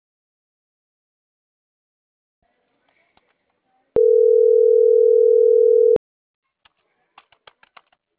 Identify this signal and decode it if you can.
Ringback tone (North America), 440 + 480 Hz.